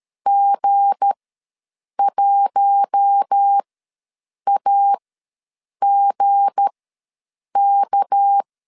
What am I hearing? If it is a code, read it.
Morse code, heard as a keyed tone, about 13 wpm, spaced as for 7 wpm, 800 Hz. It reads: G1AGK
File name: LEZIONI / 7 GF9Z / 7 G ws 12wpm 3140